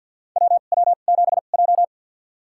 Morse code, heard as a keyed tone, about 51 wpm, spaced as for 36 wpm, 690 Hz. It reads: WW7J
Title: USA Call Signs - 2x Character Spacing - 50wpm